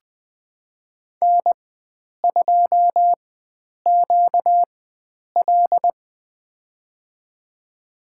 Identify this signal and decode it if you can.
Morse code, heard as a keyed tone, about 20 wpm, spaced as for 9 wpm, 700 Hz. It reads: N2QL
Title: USA Call Signs - 4 Characters - 4x Character Spacing - 20wpm